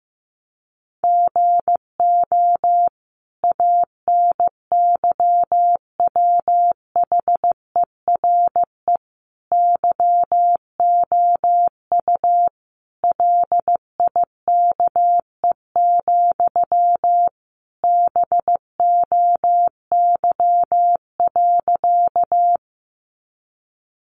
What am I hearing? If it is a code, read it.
Morse code, heard as a keyed tone, about 14 wpm, 700 Hz. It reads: GO ANYWHERE YOU LIKE, BOY.
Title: Sentences from Top 1400 Words 15wpm